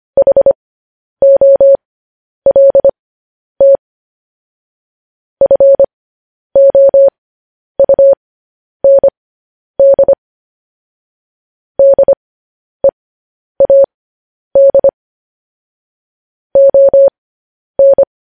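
Morse code, heard as a keyed tone, about 25 wpm, spaced as for 10 wpm, 550 Hz. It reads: HOLT FOUND DEAD ON
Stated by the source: Morse code news headlines broadcast.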